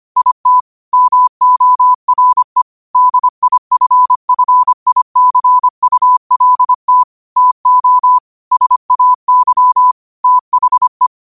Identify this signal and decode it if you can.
Morse code, heard as a keyed tone, about 25 wpm, 1000 Hz. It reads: IT MORE DIFFICULT TO SAY THE